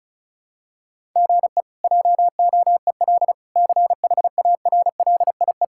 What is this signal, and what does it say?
Morse code, heard as a keyed tone, about 36 wpm, 700 Hz. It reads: GE JOEL CHARLIE
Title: K1USN SST QSO Practice - 35wpm